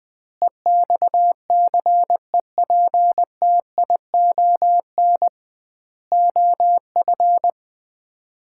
Morse code, heard as a keyed tone, about 20 wpm, 700 Hz. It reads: EXCEPTION OF
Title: 2x Word-Spacing - Sets of 4 Words - 20wpm